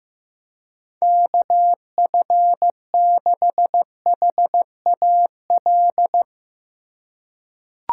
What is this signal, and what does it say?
Morse code, heard as a keyed tone, about 15 wpm, 700 Hz. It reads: KF6HAL
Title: USA Call Signs - 4x Character Spacing - 15wpm